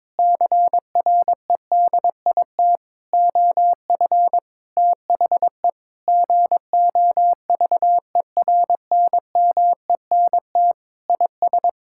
Morse code, heard as a keyed tone, about 22 wpm, 700 Hz. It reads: CREDIT OF THE GOVERNMENT IS
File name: Sentences with 7-Word Phrases - 22wpm